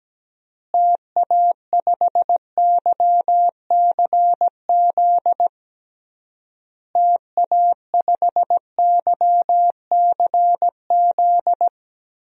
Morse code, heard as a keyed tone, about 17 wpm, 700 Hz. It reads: TA5YCZ TA5YCZ
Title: DXCC-100 Call Signs - 6 Characters - Repeated 3x - 17wpm